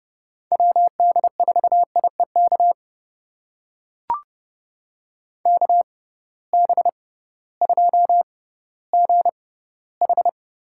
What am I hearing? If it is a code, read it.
Morse code, heard as a keyed tone, about 30 wpm, spaced as for 16 wpm, 700 Hz. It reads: WD4IEK KB2GH
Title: USA Call Signs - 6 Characters - 6x Character Spacing - 30wpm